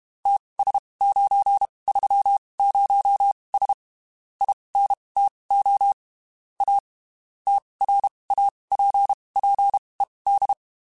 Morse code, heard as a keyed tone, about 32 wpm, spaced as for 23 wpm, 800 Hz. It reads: TS930S INTO A TRAPPED
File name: (spaced) / QSOs Spaced-mp3s / WA7QOX 2 @30 sp